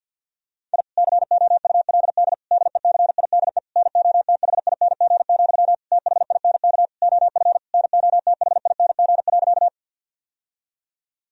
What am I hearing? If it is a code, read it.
Morse code, heard as a keyed tone, about 49 wpm, 700 Hz. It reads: I COULD DECIDE NOTHING, THINK OF NOTHING.
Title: Sentences from Top 1000 Words - Review Part 3 - 50wpm